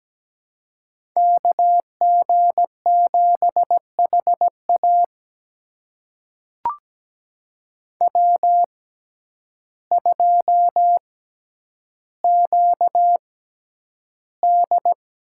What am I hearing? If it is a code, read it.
Morse code, heard as a keyed tone, about 17 wpm, spaced as for 9 wpm, 700 Hz. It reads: KG7HA W2QD
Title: USA Call Signs - 5 Characters - 6x Character Spacing - 17wpm